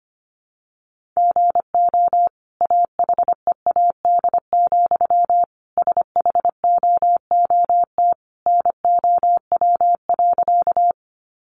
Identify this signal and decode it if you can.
Morse code, heard as a keyed tone, about 25 wpm, 700 Hz. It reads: GO AHEAD, SHOOT NOW.
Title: Sentences from Top 1200 Words 25wpm